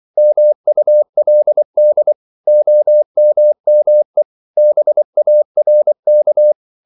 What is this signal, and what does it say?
Morse code, heard as a keyed tone, about 24 wpm, 600 Hz. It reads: MULD OMME BARK
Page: Korte danske ord 24wpm